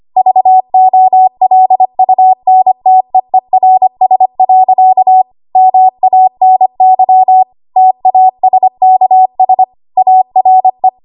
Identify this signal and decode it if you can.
Morse code, heard as a keyed tone, about 25 wpm, 750 Hz. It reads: VOLUNTEERS. MANY TASKS ARE
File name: Morse / 25WPM